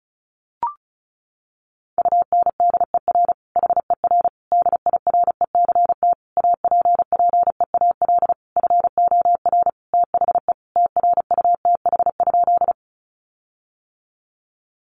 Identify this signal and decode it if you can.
Morse code, heard as a keyed tone, about 34 wpm, 700 Hz. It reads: UNDER HER DIRECT APPEAL FOR THE TRUTH?
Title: Sentences from Top 1300 Words 35wpm